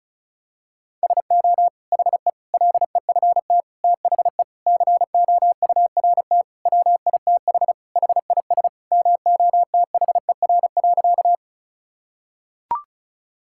Morse code, heard as a keyed tone, about 36 wpm, 700 Hz. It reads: SO HE LEFT THE COURT WITH HIS MOTHER.
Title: Sentences from Top 1600 Words 35wpm